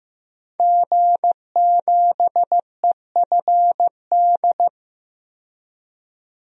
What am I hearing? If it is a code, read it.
Morse code, heard as a keyed tone, about 16 wpm, 700 Hz. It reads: G7EFD
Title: DXCC-100 Call Signs - 5 Characters - 15wpm